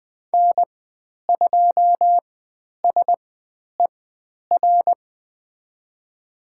Morse code, heard as a keyed tone, about 20 wpm, spaced as for 10 wpm, 700 Hz. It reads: N2SER
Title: 095 Lesson 16g - Letter c - Call Signs 20wpm